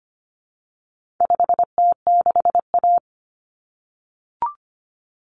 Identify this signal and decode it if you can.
Morse code, heard as a keyed tone, about 25 wpm, 700 Hz. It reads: 5T6A
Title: Common Cut Numbers - 4-Digit 25wpm